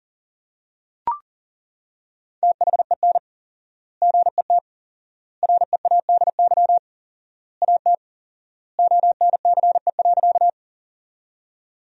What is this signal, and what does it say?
Morse code, heard as a keyed tone, about 40 wpm, 700 Hz. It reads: THEN GET READY AT ONCE.
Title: 4x Word-Spacing - Sentences from Top 400 Words - 40wpm